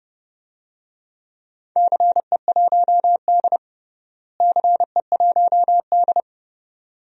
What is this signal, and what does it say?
Morse code, heard as a keyed tone, about 30 wpm, 700 Hz. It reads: CE1D CE1D
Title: DXCC-100 Call Signs - 4 Characters - Repeated 2x - 30wpm